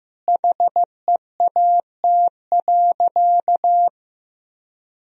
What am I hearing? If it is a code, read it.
Morse code, heard as a keyed tone, about 15 wpm, 700 Hz. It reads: HEAT.
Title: Sentences from Top 1000 Words - Review Part 3 - 15wpm